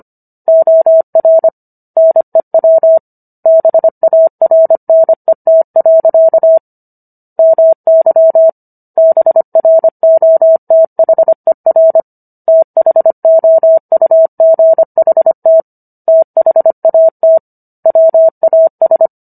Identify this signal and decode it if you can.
Morse code, heard as a keyed tone, about 25 wpm, 650 Hz. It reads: OR NEW BARNET. MY BROTHER THOUGHT THAT WAS